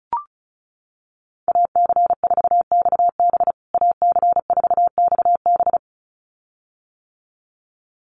Morse code, heard as a keyed tone, about 34 wpm, 700 Hz. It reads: AC4XB AC4XB
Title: USA Call Signs Repeated 2x 35wpm